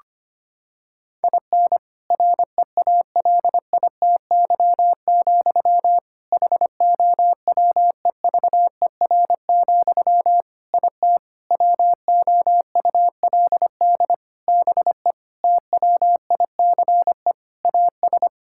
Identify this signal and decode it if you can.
Morse code, heard as a keyed tone, about 25 wpm, 700 Hz. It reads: IN REALITY, HOWEVER, IT WOULD BE TWICE AS
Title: Sentences from Top 1600 Words 25wpm